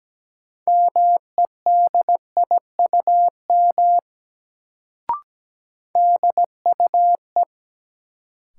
Morse code, heard as a keyed tone, about 17 wpm, 700 Hz. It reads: MEDIUM DUE
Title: Sentences from Top 900 Words - New Word Drill - 17wpm